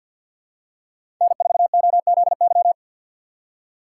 Morse code, heard as a keyed tone, about 48 wpm, 700 Hz. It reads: N4OZY
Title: USA Call Signs - 5 Characters - 3x Character Spacing - 50wpm